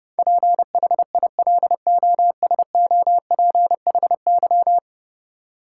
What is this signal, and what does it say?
Morse code, heard as a keyed tone, about 30 wpm, 700 Hz. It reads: PHILOSOPHY
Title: Sentences from Top 900 Words - New Word Drill - 30wpm